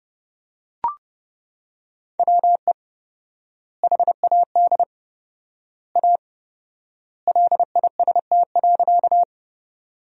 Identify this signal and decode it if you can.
Morse code, heard as a keyed tone, about 30 wpm, 700 Hz. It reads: WE HAD A LIST.
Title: 4x Word-Spacing - Sentences from Top 400 Words - 30wpm